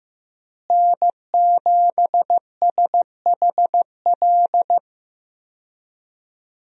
Morse code, heard as a keyed tone, about 15 wpm, 700 Hz. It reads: N7SHL
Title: USA Call Signs - 5 Characters - Repeated 2x - 15wpm